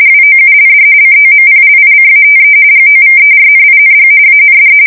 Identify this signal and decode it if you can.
RTTY, about 45.45 baud. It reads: WOULD COMPLICATE GOVERNING AND
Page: 170 Shift 2125/2295